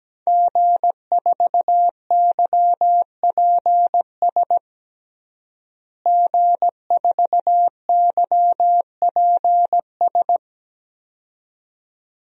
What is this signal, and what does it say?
Morse code, heard as a keyed tone, about 17 wpm, 700 Hz. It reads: G4YPS G4YPS
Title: DXCC-100 Call Signs - 5 Characters - Repeated 2x - 17wpm